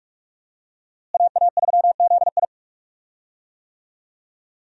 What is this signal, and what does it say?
Morse code, heard as a keyed tone, about 45 wpm, 700 Hz. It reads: AA3ZI
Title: USA Call Signs - 5 Characters - Repeated 3x - 45wpm